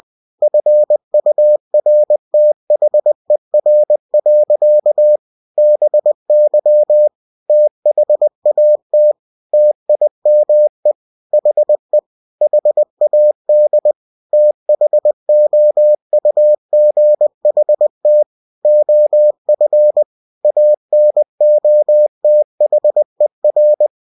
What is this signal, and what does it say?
Morse code, heard as a keyed tone, about 20 wpm, 600 Hz. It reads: FURTHER. BY THAT TIME HE HAD THOUGHT OF ANOTHER
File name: cw-books / winnie-the-pooh / winnie-20wpm-page-0003